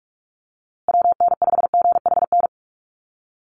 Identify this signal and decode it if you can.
Morse code, heard as a keyed tone, about 44 wpm, 700 Hz. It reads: WN5GHN
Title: USA Call Signs Repeated 2x 45wpm